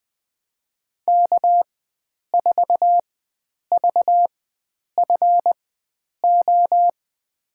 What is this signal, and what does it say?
Morse code, heard as a keyed tone, about 21 wpm, spaced as for 9 wpm, 700 Hz. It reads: K4VFO
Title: USA Call Signs - 5 Characters - 4x Character Spacing - 20wpm